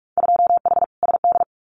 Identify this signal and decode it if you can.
Morse code, heard as a keyed tone, about 44 wpm, 700 Hz. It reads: 2H SD
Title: Winter Field Day Exchanges 45wpm